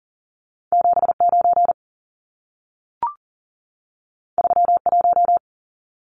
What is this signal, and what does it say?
Morse code, heard as a keyed tone, about 40 wpm, 700 Hz. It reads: 79 31
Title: Numbers - 2-Digit 40wpm